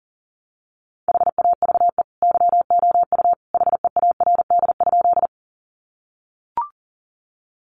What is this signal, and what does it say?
Morse code, heard as a keyed tone, about 40 wpm, 700 Hz. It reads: HAVE YOU HEARD?
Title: Sentences from Top 400 Words 40wpm